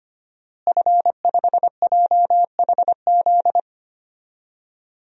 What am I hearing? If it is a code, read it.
Morse code, heard as a keyed tone, about 25 wpm, 700 Hz. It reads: F5JHZ
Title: DXCC-100 Call Signs - 5 Characters - Repeated 2x - 25wpm